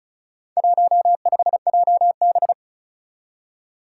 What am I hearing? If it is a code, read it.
Morse code, heard as a keyed tone, about 36 wpm, 700 Hz. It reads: 15JB
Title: Four Challenging Letter-Number Combinations 35wpm